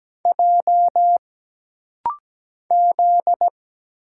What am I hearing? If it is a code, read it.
Morse code, heard as a keyed tone, about 17 wpm, 700 Hz. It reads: J Z
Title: Single Letters - Rapid-Fire 17wpm